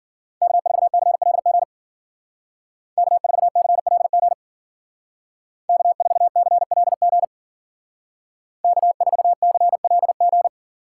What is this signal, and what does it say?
Morse code, heard as a keyed tone, about 49 wpm, 700 Hz. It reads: K4CLG K4CLG K4CLG K4CLG
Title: Speed Racing - USA Call Signs 40wpm